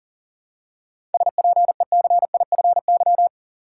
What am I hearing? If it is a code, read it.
Morse code, heard as a keyed tone, about 40 wpm, 700 Hz. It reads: SPECIFY
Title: Seven Letter Words - 40wpm